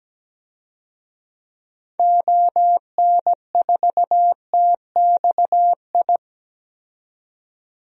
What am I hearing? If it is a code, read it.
Morse code, heard as a keyed tone, about 17 wpm, 700 Hz. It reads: ON4TXI
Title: DXCC-100 Call Signs - 6 Characters - 17wpm